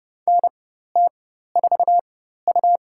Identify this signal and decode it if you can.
Morse code, heard as a keyed tone, about 31 wpm, spaced as for 14 wpm, 700 Hz. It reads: NT4U
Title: USA Call Signs - 4 Characters - 4x Character Spacing - 30wpm